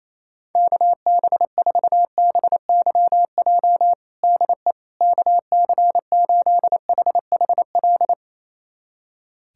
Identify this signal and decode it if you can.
Morse code, heard as a keyed tone, about 28 wpm, 700 Hz. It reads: KB4BYJ DE KC8HHL
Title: USA Callsign DE Callsign 28wpm